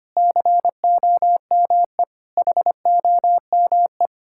Morse code, heard as a keyed tone, about 25 wpm, 700 Hz. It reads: COME HOME
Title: Sentences from Top 1000 Words - Review Part 5 - 25wpm